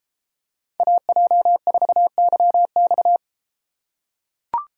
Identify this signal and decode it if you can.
Morse code, heard as a keyed tone, about 31 wpm, 700 Hz. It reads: AJ4YX